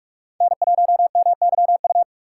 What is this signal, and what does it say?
Morse code, heard as a keyed tone, about 45 wpm, 700 Hz. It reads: N1MYU